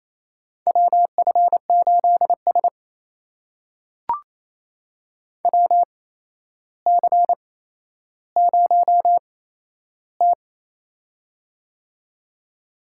Morse code, heard as a keyed tone, about 28 wpm, spaced as for 12 wpm, 700 Hz. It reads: WF8S WC0T
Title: USA Call Signs - 4 Characters - 8x Character Spacing - 28wpm